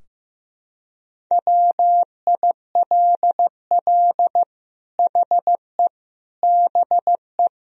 Morse code, heard as a keyed tone, about 15 wpm, 700 Hz. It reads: WILL HE BE
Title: Sentences from Top 1000 Words - Review Part 3 - 15wpm